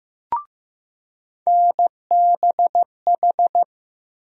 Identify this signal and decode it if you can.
Morse code, heard as a keyed tone, about 15 wpm, 700 Hz. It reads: NBH
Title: USA Call Sign Postfixes - 3 Letters 15wpm